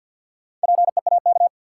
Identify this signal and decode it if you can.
Morse code, heard as a keyed tone, about 52 wpm, 700 Hz. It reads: PEAK
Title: Four Letter Words - 50wpm